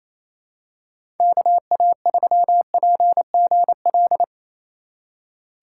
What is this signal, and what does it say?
Morse code, heard as a keyed tone, about 28 wpm, 700 Hz. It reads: KA3PGL
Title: USA Call Signs - 6 Characters - 4x Character Spacing - 28wpm